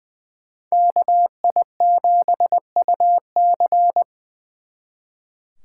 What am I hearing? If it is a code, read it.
Morse code, heard as a keyed tone, about 23 wpm, 700 Hz. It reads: KI7UC
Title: Speed Racing - USA 2x2 Call Signs 20wpm